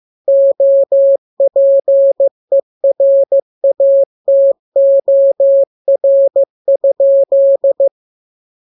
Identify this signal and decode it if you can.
Morse code, heard as a keyed tone, about 15 wpm, 550 Hz. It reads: OPERATOR?